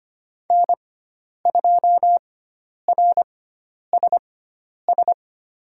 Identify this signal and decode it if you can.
Morse code, heard as a keyed tone, about 25 wpm, spaced as for 10 wpm, 700 Hz. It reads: N2RSS
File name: Lesson 16g - Letter c - Call Signs 25wpm